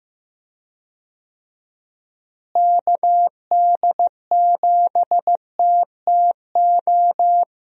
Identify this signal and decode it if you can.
Morse code, heard as a keyed tone, about 15 wpm, 700 Hz. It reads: KD7TTO
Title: USA Call Signs - 6 Characters - Repeated 3x - 15wpm